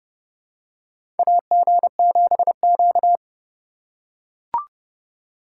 Morse code, heard as a keyed tone, about 29 wpm, 700 Hz. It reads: AG7Q
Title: USA Call Signs - 4 Characters - 4x Character Spacing - 30wpm